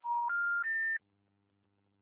Special information tone.